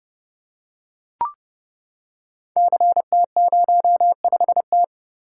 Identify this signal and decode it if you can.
Morse code, heard as a keyed tone, about 30 wpm, 700 Hz. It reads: CT05T
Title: Code Groups - 5 Letter:Numbers - 30wpm